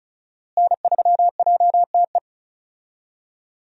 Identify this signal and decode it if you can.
Morse code, heard as a keyed tone, about 34 wpm, 700 Hz. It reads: N3JTE
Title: USA Call Signs - 5 Characters - Repeated 3x - 35wpm